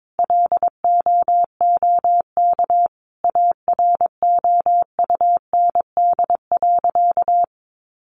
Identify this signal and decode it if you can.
Morse code, heard as a keyed tone, about 22 wpm, 700 Hz. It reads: LOOK AROUND.
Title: Sentences from Top 800 Words 22wpm